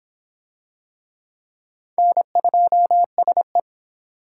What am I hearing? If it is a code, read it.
Morse code, heard as a keyed tone, about 22 wpm, 700 Hz. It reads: N2SE